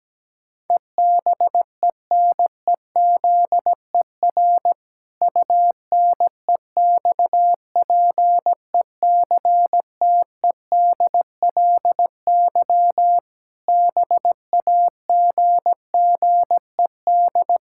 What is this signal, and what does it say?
Morse code, heard as a keyed tone, about 17 wpm, 700 Hz. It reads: EBENEZER UNEXPECTEDLY BAGGED
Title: Pangram Challenge v2 - 17wpm